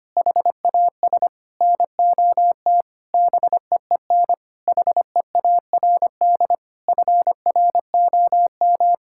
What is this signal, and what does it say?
Morse code, heard as a keyed tone, about 25 wpm, 700 Hz. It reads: HAS NOT BEEN HEARD FROM
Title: Sentences from Top 1600 Words 25wpm